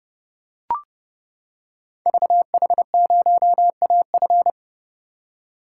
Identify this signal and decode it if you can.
Morse code, heard as a keyed tone, about 30 wpm, 700 Hz. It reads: VH0AF
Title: Code Groups - 5 Letter:Numbers - 30wpm